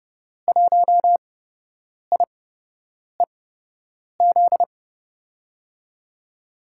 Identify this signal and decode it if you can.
Morse code, heard as a keyed tone, about 31 wpm, spaced as for 8 wpm, 700 Hz. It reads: 1IEZ
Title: USA Call Signs - 6 Characters - 8x Character Spacing - 30wpm